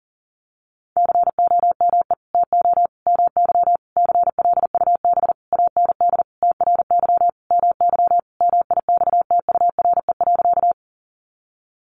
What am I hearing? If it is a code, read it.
Morse code, heard as a keyed tone, about 40 wpm, 700 Hz. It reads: COME TO MY CLUB AND TRY MY MIXTURE.
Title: Sentences from Top 1500 Words 40wpm